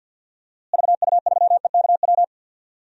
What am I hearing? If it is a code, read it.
Morse code, heard as a keyed tone, about 49 wpm, 700 Hz. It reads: VU3EKW
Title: DXCC-100 Call Signs - 6 Characters - Repeated 3x - 50wpm